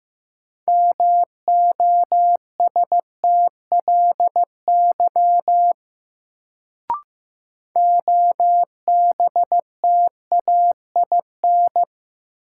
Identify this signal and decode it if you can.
Morse code, heard as a keyed tone, about 15 wpm, 700 Hz. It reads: MOSTLY OBTAIN
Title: Sentences from Top 1300 Words - New Word Drill Alt - 15wpm